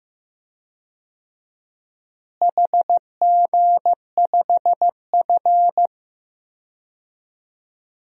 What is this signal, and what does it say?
Morse code, heard as a keyed tone, about 16 wpm, 700 Hz. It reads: HG5F
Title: DXCC-100 Call Signs - 4 Characters - Repeated 3x - 15wpm